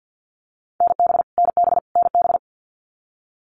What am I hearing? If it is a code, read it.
Morse code, heard as a keyed tone, about 50 wpm, 700 Hz. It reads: NB NB NB
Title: Canadian Provinces and Territories - Repeated 3x - 50wpm